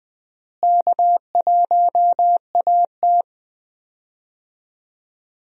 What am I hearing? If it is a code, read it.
Morse code, heard as a keyed tone, about 20 wpm, 700 Hz. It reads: K1AT
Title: USA Call Signs - 4 Characters - Repeated 3x - 20wpm